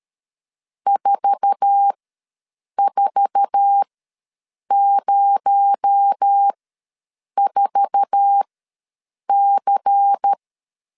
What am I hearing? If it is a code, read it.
Morse code, heard as a keyed tone, about 13 wpm, spaced as for 7 wpm, 800 Hz. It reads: 4404C
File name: LEZIONI / 8 YJ45 / 8 YJ4 ws 12wpm 3249